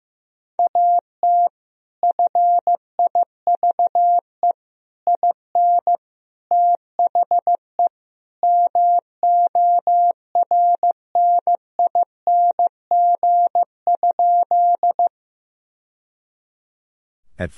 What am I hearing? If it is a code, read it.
Morse code, heard as a keyed tone, about 16 wpm, 700 Hz. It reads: AT FIVE IN THE MORNING?
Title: Sentences from Top 1000 Words - Review Part 2 - 15wpm